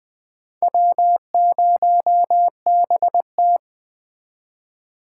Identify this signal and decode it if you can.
Morse code, heard as a keyed tone, about 20 wpm, 700 Hz. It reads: W0BT
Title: Speed Racing - USA 1x2 Call Signs 17wpm